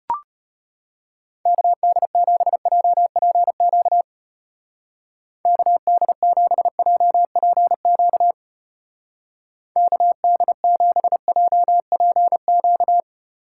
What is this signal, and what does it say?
Morse code, heard as a keyed tone, about 31 wpm, 700 Hz. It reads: KD7JPQ KD7JPQ KD7JPQ
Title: Speed Racing - USA 2x3 Call Signs 25wpm